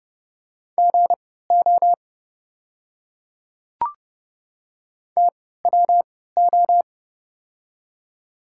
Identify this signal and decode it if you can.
Morse code, heard as a keyed tone, about 30 wpm, spaced as for 17 wpm, 700 Hz. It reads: GO TWO
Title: 3x Character Spacing - Top 200 Words - 30wpm